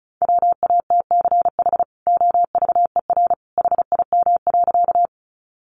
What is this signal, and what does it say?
Morse code, heard as a keyed tone, about 34 wpm, 700 Hz. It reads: WATCH OVER HIM.
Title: Sentences from Top 300 Words 35wpm